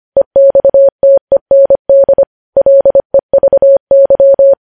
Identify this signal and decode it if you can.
Morse code, heard as a keyed tone, about 25 wpm, 550 Hz. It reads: EXTEND LEVY